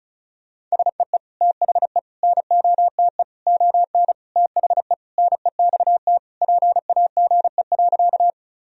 Morse code, heard as a keyed tone, about 36 wpm, 700 Hz. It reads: SEE THE NOTE ON THE NEXT PAGE.
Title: Sentences from Top 1000 Words - Review Part 2 - 35wpm